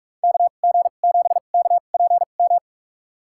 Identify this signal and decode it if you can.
Morse code, heard as a keyed tone, about 46 wpm, spaced as for 32 wpm, 700 Hz. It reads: KG7KPM
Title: USA Call Signs - 6 Characters - 2x Character Spacing - 45wpm